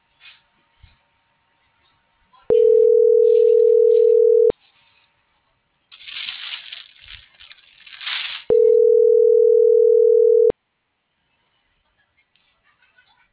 Ringback tone (North America), 440 + 480 Hz.